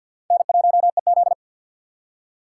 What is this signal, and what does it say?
Morse code, heard as a keyed tone, about 52 wpm, 700 Hz. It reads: N1EZ
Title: USA Call Signs - 4x Character Spacing - 50wpm